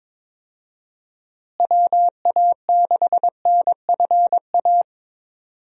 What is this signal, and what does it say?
Morse code, heard as a keyed tone, about 22 wpm, 700 Hz. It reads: WA6NFA
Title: USA Call Signs - 6 Characters - Repeated 2x - 22wpm